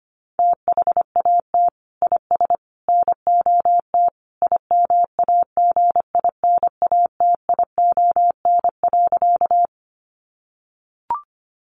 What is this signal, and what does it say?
Morse code, heard as a keyed tone, about 25 wpm, 700 Hz. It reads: THAT IS NOT IMAGINATION.
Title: Sentences from Top 1300 Words 25wpm